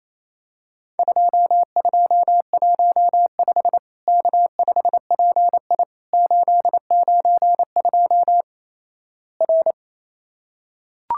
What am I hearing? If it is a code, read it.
Morse code, heard as a keyed tone, about 28 wpm, 700 Hz. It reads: 2215 K5PI 892 R
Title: Worked All Europe DX Contest - QTC Exchange - 28wpm